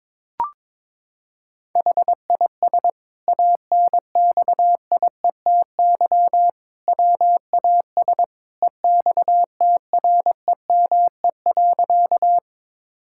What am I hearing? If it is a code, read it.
Morse code, heard as a keyed tone, about 21 wpm, 700 Hz. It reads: HIS ANXIETY WAS EXTREME.
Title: Sentences from Top 1600 Words 22wpm